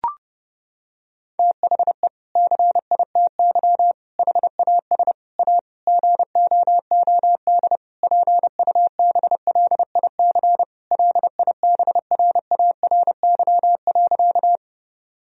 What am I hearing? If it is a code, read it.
Morse code, heard as a keyed tone, about 32 wpm, 700 Hz. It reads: THE CITY HAS A GOOD PUBLIC LIBRARY.
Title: Sentences from Top 1000 Words - Review Part 4 - 30wpm